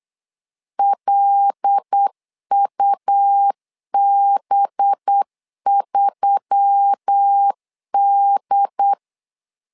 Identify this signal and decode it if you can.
Morse code, heard as a keyed tone, about 8 wpm, 800 Hz. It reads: LUB3D
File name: LEZIONI / 5 LBVC / 5 LB 8wpm 2816